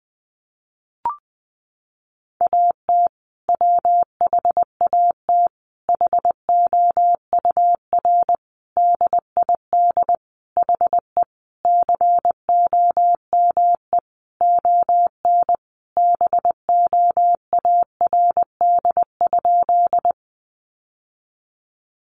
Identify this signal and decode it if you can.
Morse code, heard as a keyed tone, about 20 wpm, 700 Hz. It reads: AT WHAT HOUR DID HE COME ON BOARD?
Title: Sentences from Top 700 Words 20wpm